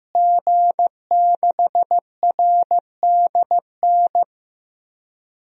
Morse code, heard as a keyed tone, about 15 wpm, 700 Hz. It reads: G6RDN
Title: DXCC-100 Call Signs - 5 Characters - Repeated 2x - 15wpm